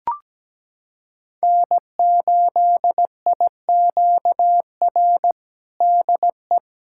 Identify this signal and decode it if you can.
Morse code, heard as a keyed tone, about 17 wpm, 700 Hz. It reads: N8IQR DE
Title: USA Callsign DE Callsign - 5 Characters - 17wpm